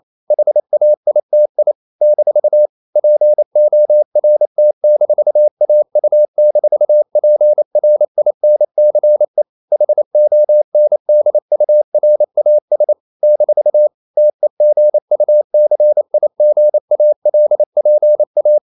Morse code, heard as a keyed tone, about 28 wpm, 600 Hz. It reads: HAITI - PORT-AU-PRINCE HONDURAS - TEGUCIGALPA
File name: CW / Practice / NationalCapitals28